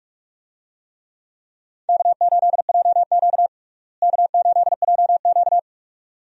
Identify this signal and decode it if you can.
Morse code, heard as a keyed tone, about 41 wpm, 700 Hz. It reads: K8JQ K8JQ
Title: USA Call Signs - 4 Characters - Repeated 2x - 45wpm